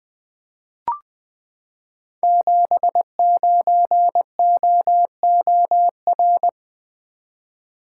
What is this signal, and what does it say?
Morse code, heard as a keyed tone, about 21 wpm, 700 Hz. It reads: 79OOR
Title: Code Groups - 5 Letter:Numbers - 20wpm